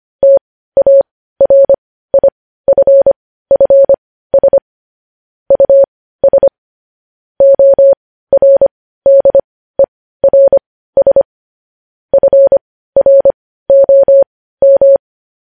Morse code, heard as a keyed tone, about 25 wpm, spaced as for 15 wpm, 550 Hz. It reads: TARIFFS US ORDERS FROM
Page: news headlines in morse code at 15 wpm